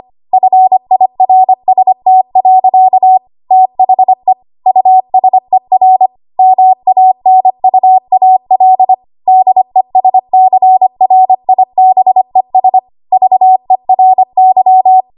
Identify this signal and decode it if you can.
Morse code, heard as a keyed tone, about 25 wpm, 750 Hz. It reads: FIRST. THE USER MANUAL DESCRIBES VERY
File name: Archive / 25 WPM / 25WPM